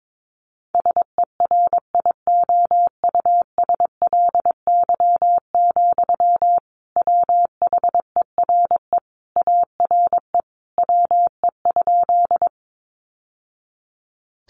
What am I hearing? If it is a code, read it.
Morse code, heard as a keyed tone, about 22 wpm, 700 Hz. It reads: SERIOUSLY, WHERE ARE WE?
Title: Sentences from Top 1200 Words 22wpm